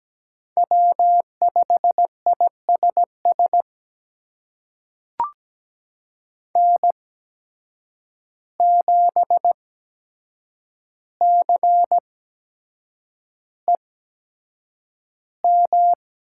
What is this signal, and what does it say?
Morse code, heard as a keyed tone, about 17 wpm, spaced as for 7 wpm, 700 Hz. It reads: W5ISS N7CEM
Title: USA Call Signs - 5 Characters - 8x Character Spacing - 17wpm